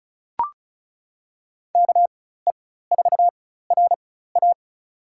Morse code, heard as a keyed tone, about 36 wpm, spaced as for 16 wpm, 700 Hz. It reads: KE4RA
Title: USA Call Signs - 5 Characters - 4x Character Spacing - 35wpm